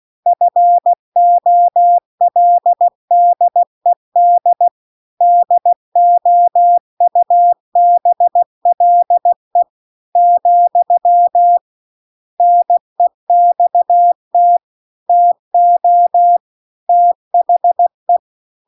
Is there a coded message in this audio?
Yes: Morse code from 0.3 to 18.2 s, heard as a keyed tone, about 16 wpm, 700 Hz. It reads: FOLDED DOUBLE , NEXT TO THE